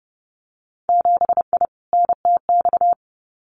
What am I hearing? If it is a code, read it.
Morse code, heard as a keyed tone, about 30 wpm, 700 Hz. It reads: 7I NTX
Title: Winter Field Day Exchanges Encore 30wpm